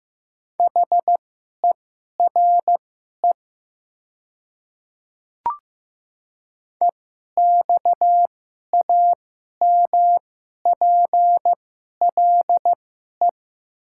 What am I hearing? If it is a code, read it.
Morse code, heard as a keyed tone, about 15 wpm, spaced as for 11 wpm, 700 Hz. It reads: HERE EXAMPLE
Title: 2x Character Spacing - Top 500 Words - 15wpm